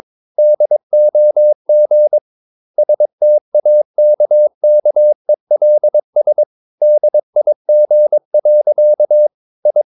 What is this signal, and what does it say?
Morse code, heard as a keyed tone, about 22 wpm, 600 Hz. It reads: DOG STAKKELS DIG. I
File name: Ferie QRS 22wpm